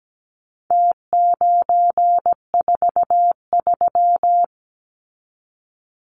Morse code, heard as a keyed tone, about 17 wpm, 700 Hz. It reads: T943
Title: Numbers - Contest Style - 4-Digit 17wpm